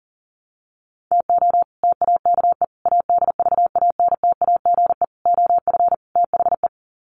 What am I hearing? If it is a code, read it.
Morse code, heard as a keyed tone, about 40 wpm, 700 Hz. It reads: TO TAKE ADVANTAGE OF THE
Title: Sets of 5 Words 40wpm